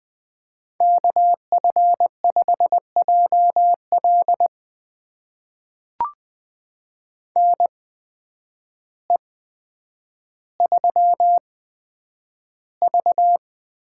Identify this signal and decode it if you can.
Morse code, heard as a keyed tone, about 20 wpm, spaced as for 9 wpm, 700 Hz. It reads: KF5JL NE3V
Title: USA Call Signs - 8x Character Spacing - 20wpm